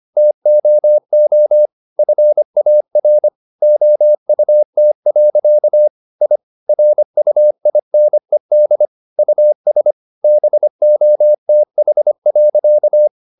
Morse code, heard as a keyed tone, about 25 wpm, 600 Hz. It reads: TOO FAR OUT. I RUINED US BOTH.